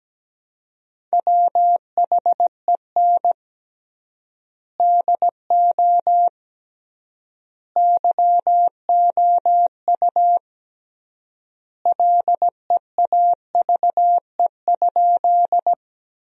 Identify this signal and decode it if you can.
Morse code, heard as a keyed tone, about 17 wpm, 700 Hz. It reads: WHEN DO YOU LEAVE?
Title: 3x Word-Spacing - Sentences from Top 400 Words - 17wpm